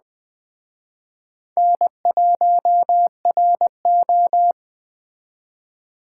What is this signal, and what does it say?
Morse code, heard as a keyed tone, about 21 wpm, 700 Hz. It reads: N1RO
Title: Speed Racing - USA 1x2 Call Signs 17wpm